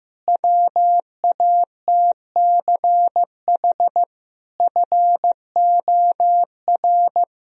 Morse code, heard as a keyed tone, about 15 wpm, 700 Hz. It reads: WATCH FOR
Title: Sentences from Top 1000 Words - Review Part 1 - 15wpm